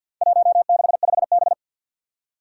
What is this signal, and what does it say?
Morse code, heard as a keyed tone, about 52 wpm, 700 Hz. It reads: 165B
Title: Four Challenging Letter-Number Combinations 50wpm